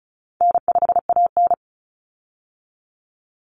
Morse code, heard as a keyed tone, about 34 wpm, 700 Hz. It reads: N5AN
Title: Call Signs Repeated 3x - 35wpm